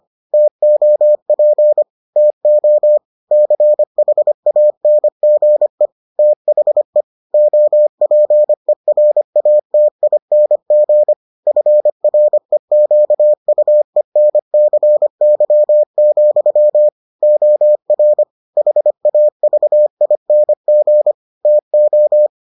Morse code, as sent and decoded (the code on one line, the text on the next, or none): - --- .--. / - --- / -.-. .... .- -. --. / - .... / --- .--. .-. .- - .. -. --. / ..-. .-. --.- ..- . -. -.-. -.-- --..-- / --- .-. / .... .- ...- .. -. --. / - ---
TOP TO CHANGE THE OPERATING FREQUENCY, OR HAVING TO